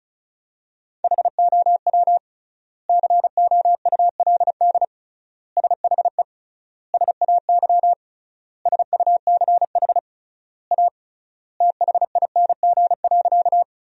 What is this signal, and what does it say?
Morse code, heard as a keyed tone, about 36 wpm, 700 Hz. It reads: HOW COULD SHE SAY SUCH A THING.
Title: 3x Word-Spacing - Sentences from Top 200 Words - 35wpm